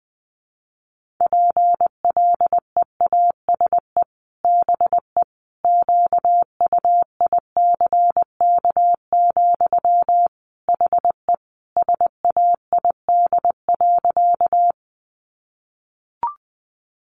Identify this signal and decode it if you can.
Morse code, heard as a keyed tone, about 20 wpm, 700 Hz. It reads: PLEASE BE QUICK, HE SAID.
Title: Sentences from Top 1500 Words 20wpm